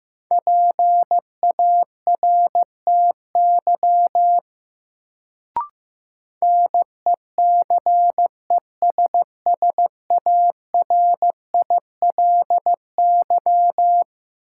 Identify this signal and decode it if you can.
Morse code, heard as a keyed tone, about 15 wpm, 700 Hz. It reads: PARTY NECESSARILY
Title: Sentences from Top 1400 Words - New Word Drill - 15wpm